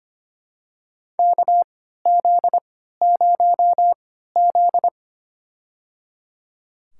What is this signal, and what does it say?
Morse code, heard as a keyed tone, about 25 wpm, spaced as for 14 wpm, 700 Hz. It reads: KZ0Z
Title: USA Call Signs - 4 Characters - 3x Character Spacing - 25wpm